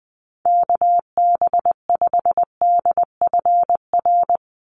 Morse code, heard as a keyed tone, about 20 wpm, 700 Hz. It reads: KB5DFR
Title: USA Call Signs 20wpm